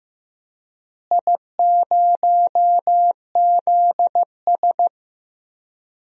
Morse code, heard as a keyed tone, about 15 wpm, 700 Hz. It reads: I0ZS
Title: DXCC-100 Call Signs - 4 Characters - Repeated 3x - 15wpm